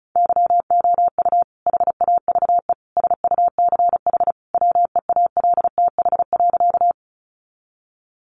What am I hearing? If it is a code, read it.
Morse code, heard as a keyed tone, about 34 wpm, 700 Hz. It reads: YOU HAVE SUCH WEALTH.
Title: Sentences from Top 1200 Words 35wpm